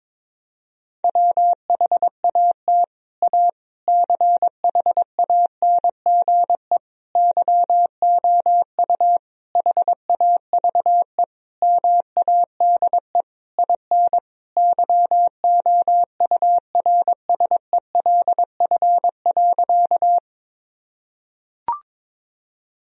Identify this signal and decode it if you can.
Morse code, heard as a keyed tone, about 22 wpm, 700 Hz. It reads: WHAT A CHANGE YOU HAVE MADE IN YOURSELF.
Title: Sentences from Top 1000 Words - Review Part 1 - 22wpm